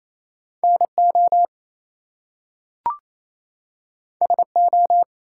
Morse code, heard as a keyed tone, about 27 wpm, 700 Hz. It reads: NO SO
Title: Two Letter Words - 28wpm